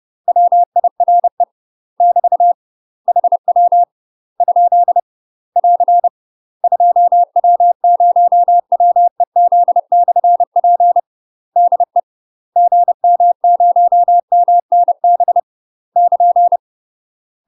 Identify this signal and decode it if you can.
Morse code, heard as a keyed tone, about 30 wpm, 700 Hz. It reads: WIRE <BT> HW ? <AR> 2W0WEZ/P DE GM0MNB <KN>
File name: UK MASTER QSOs TRAINING / 30WPM / UK MASTER QSOs 0029